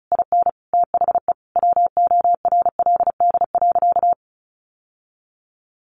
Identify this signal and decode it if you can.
Morse code, heard as a keyed tone, about 34 wpm, 700 Hz. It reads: IN THE WORLD.
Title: Sentences from Top 1300 Words 35wpm